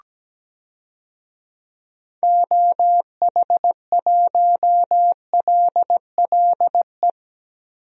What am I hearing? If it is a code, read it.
Morse code, heard as a keyed tone, about 17 wpm, 700 Hz. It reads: OH1LLE
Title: DXCC-100 Call Signs - 6 Characters - Repeated 2x - 17wpm